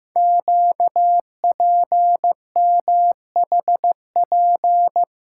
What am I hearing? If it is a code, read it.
Morse code, heard as a keyed tone, about 15 wpm, 700 Hz. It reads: QPMHP
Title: Code Groups - 5 Letters - 15wpm